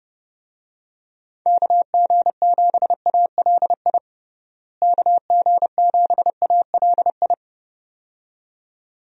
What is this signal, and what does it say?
Morse code, heard as a keyed tone, about 30 wpm, 700 Hz. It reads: KG7ALI KG7ALI
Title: USA Call Signs - 6 Characters - Repeated 2x - 30wpm